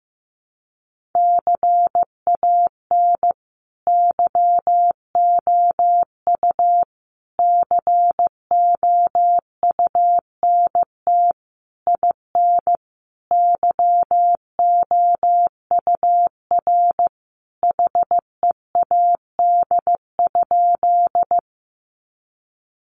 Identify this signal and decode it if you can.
Morse code, heard as a keyed tone, about 15 wpm, 700 Hz. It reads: CAN YOU COUNT IN YOUR HEAD?
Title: Sentences from Top 1500 Words 15wpm